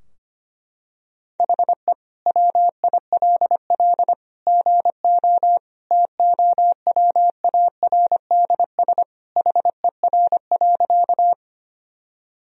Morse code, heard as a keyed tone, about 25 wpm, 700 Hz. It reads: HE WILL GO TOWARDS HER.
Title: Sentences from Top 1000 Words - Review Part 2 - 25wpm